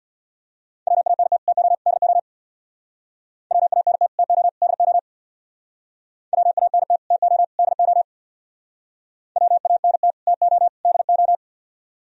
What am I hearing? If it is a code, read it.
Morse code, heard as a keyed tone, about 63 wpm, 700 Hz. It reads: WANT TO DO WANT TO DO WANT TO DO WANT TO DO